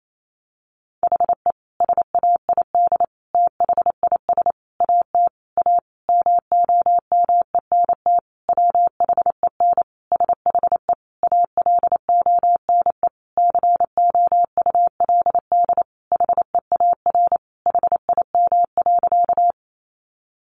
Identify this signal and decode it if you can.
Morse code, heard as a keyed tone, about 28 wpm, 700 Hz. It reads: HE SAID THIS AT A MOMENT WHEN SHE ALONE COULD HEAR HIM.
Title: Sentences from Top 1000 Words 28wpm